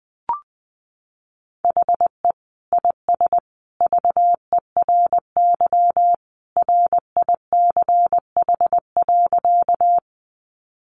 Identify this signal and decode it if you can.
Morse code, heard as a keyed tone, about 20 wpm, 700 Hz. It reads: HE IS VERY RICH.
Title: Sentences from Top 1100 Words 20wpm